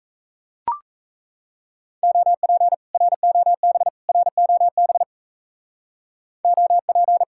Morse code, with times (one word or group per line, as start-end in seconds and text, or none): 2.0-2.7 OP
2.9-3.9 ROB
4.1-5.0 ROB
6.4-7.2 OP